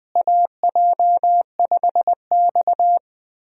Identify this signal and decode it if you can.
Morse code, heard as a keyed tone, about 21 wpm, 700 Hz. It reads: AJ5X
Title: USA Call Signs - 4 Characters - Repeated 2x - 20wpm